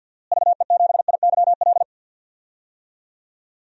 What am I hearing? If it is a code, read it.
Morse code, heard as a keyed tone, about 52 wpm, 700 Hz. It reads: VE7IYL
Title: DXCC-100 Call Signs - 6 Characters - Repeated 3x - 50wpm